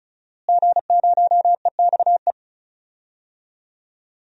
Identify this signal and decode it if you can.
Morse code, heard as a keyed tone, about 36 wpm, 700 Hz. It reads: G0EXE
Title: DXCC-100 Call Signs - 5 Characters - Repeated 2x - 35wpm